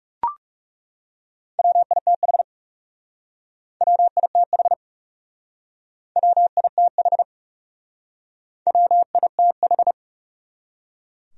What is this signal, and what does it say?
Morse code, heard as a keyed tone, about 36 wpm, 700 Hz. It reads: WITH WITH WITH WITH
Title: Speed-Racing Top 500 Words 30wpm